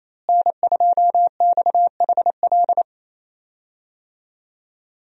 Morse code, heard as a keyed tone, about 28 wpm, 700 Hz. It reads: N2XHL
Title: USA Call Signs - 5 Characters - Repeated 2x - 28wpm